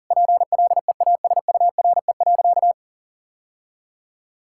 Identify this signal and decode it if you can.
Morse code, heard as a keyed tone, about 40 wpm, 700 Hz. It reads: PLEASURE.
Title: Sentences from Top 1600 Words 40wpm